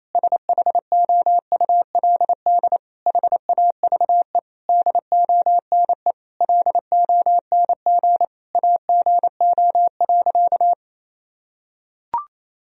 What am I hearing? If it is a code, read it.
Morse code, heard as a keyed tone, about 27 wpm, 700 Hz. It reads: SHOULD HAVE DONE LONG AGO.
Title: Sentences from Top 1000 Words - Review Part 3 - 28wpm